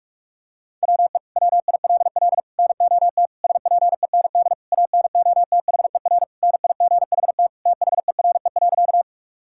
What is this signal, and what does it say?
Morse code, heard as a keyed tone, about 42 wpm, 700 Hz. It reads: WE WILL NOT SPEND ANOTHER NIGHT THERE.
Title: Sentences from Top 1000 Words - Review Part 5 - 45wpm